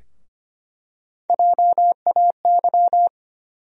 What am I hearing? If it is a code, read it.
Morse code, heard as a keyed tone, about 25 wpm, 700 Hz. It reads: JAY
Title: USA Call Sign Postfixes - 3 Letters 25wpm